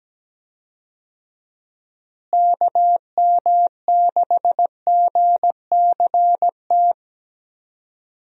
Morse code, heard as a keyed tone, about 17 wpm, 700 Hz. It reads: KM6GCT
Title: USA Call Signs - 6 Characters - Repeated 2x - 17wpm